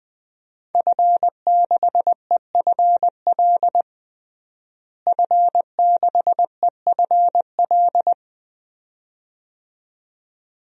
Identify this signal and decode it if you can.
Morse code, heard as a keyed tone, about 21 wpm, 700 Hz. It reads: F6EFL F6EFL
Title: DXCC-100 Call Signs - 5 Characters - Repeated 2x - 20wpm